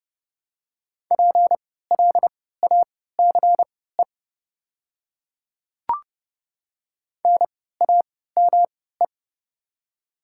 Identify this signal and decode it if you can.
Morse code, heard as a keyed tone, about 30 wpm, spaced as for 17 wpm, 700 Hz. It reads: PLACE NAME
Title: 3x Character Spacing - Top 300 Words - 30wpm